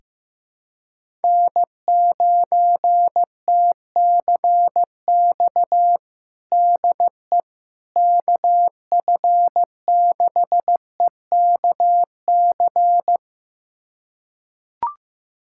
Morse code, heard as a keyed tone, about 15 wpm, 700 Hz. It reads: N9TCX DE KF6EKC
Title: USA Callsign DE Callsign 15wpm